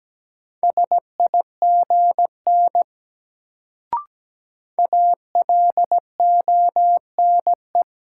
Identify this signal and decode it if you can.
Morse code, heard as a keyed tone, about 17 wpm, 700 Hz. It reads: SIGN ALONE
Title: Sentences from Top 1000 Words - New Word Drill - 17wpm